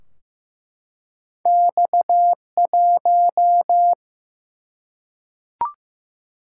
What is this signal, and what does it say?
Morse code, heard as a keyed tone, about 15 wpm, 700 Hz. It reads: X1
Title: Single-Letter Single-Number 15wpm